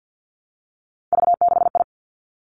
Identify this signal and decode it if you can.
Morse code, heard as a keyed tone, about 50 wpm, 700 Hz. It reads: V6I
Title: Three Challenging Letter-Number Combinations 50wpm